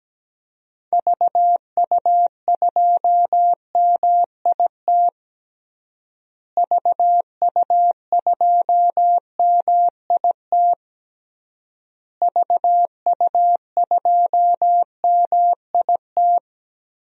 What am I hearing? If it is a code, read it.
Morse code, heard as a keyed tone, about 17 wpm, 700 Hz. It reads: VU2MIT VU2MIT VU2MIT
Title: DXCC-100 Call Signs - 6 Characters - Repeated 3x - 17wpm